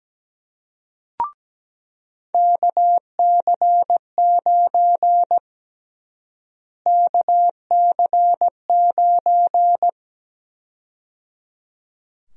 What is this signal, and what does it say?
Morse code, heard as a keyed tone, about 17 wpm, 700 Hz. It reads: KC9 KC9
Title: USA Call Sign Prefixes - Repeated 2x - 17wpm